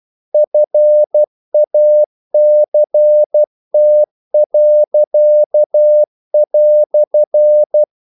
Morse code, heard as a keyed tone, about 12 wpm, 600 Hz. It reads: FACT."
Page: English quotations in CW (telegraphy - morse)